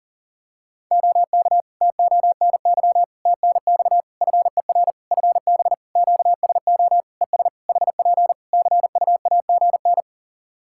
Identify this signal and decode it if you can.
Morse code, heard as a keyed tone, about 40 wpm, 700 Hz. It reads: OK TONY TNX FER FB QSO ES HP CUAGN
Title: Practice QSO - Exchange 4 - Ending - 40wpm